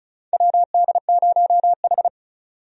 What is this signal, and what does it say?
Morse code, heard as a keyed tone, about 34 wpm, 700 Hz. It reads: WD0H
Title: USA Call Signs - 4 Characters - 35wpm